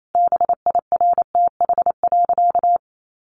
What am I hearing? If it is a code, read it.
Morse code, heard as a keyed tone, about 28 wpm, 700 Hz. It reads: BIRTH.
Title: Sentences from Top 900 Words 28wpm